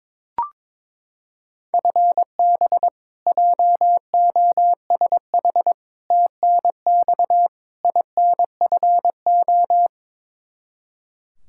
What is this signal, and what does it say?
Morse code, heard as a keyed tone, about 22 wpm, 700 Hz. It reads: FB JOSH TNX INFO
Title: Practice QSO - Exchange 3 - More Info - 22wpm